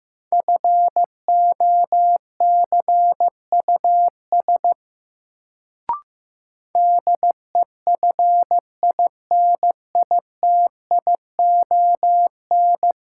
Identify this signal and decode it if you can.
Morse code, heard as a keyed tone, about 15 wpm, 700 Hz. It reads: FOCUS DEFINITION
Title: Sentences from Top 1000 Words - Review Part 4 - New Word Drill - 15wpm